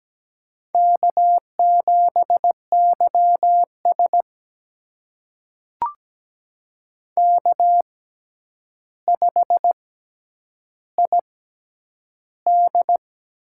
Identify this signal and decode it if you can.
Morse code, heard as a keyed tone, about 17 wpm, spaced as for 9 wpm, 700 Hz. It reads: K7YS K5ID
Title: USA Call Signs - 4 Characters - 6x Character Spacing - 17wpm